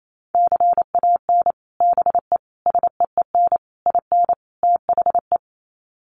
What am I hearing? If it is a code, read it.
Morse code, heard as a keyed tone, about 28 wpm, 700 Hz. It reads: CAN BE SEEN IN THE
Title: Sets of 5 Words Encore2 28wpm